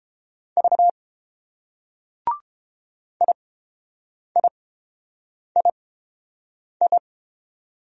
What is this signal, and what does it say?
Morse code, heard as a keyed tone, about 30 wpm, spaced as for 7 wpm, 690 Hz. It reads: V IIII